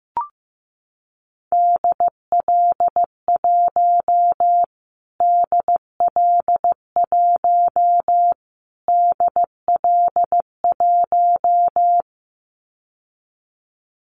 Morse code, heard as a keyed tone, about 15 wpm, 700 Hz. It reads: DL1 DL1 DL1
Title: Callsign Prefixes for Top 25 DXCC - Repeated 3x - 15wpm